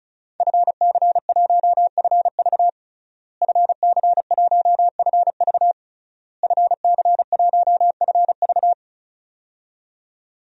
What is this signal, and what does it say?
Morse code, heard as a keyed tone, about 34 wpm, 700 Hz. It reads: FC1FV FC1FV FC1FV
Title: DXCC-100 Call Signs - 5 Characters - Repeated 3x - 35wpm